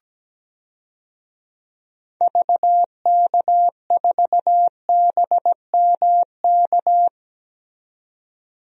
Morse code, heard as a keyed tone, about 17 wpm, 700 Hz. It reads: VK4BMK
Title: DXCC-100 Call Signs - 6 Characters - 17wpm